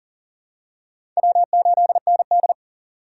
Morse code, heard as a keyed tone, about 40 wpm, 700 Hz. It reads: W8ND
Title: USA Call Signs - 4 Characters - 4x Character Spacing - 40wpm